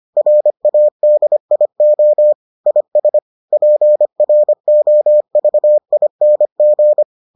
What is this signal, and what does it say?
Morse code, heard as a keyed tone, about 25 wpm, 600 Hz. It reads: RADIO IS PROVING